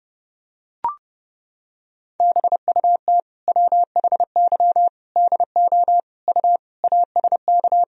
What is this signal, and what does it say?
Morse code, heard as a keyed tone, about 30 wpm, 700 Hz. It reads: BUT WHY DO U ASK
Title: Sentences from Top 200 Words and Abbreviations 30wpm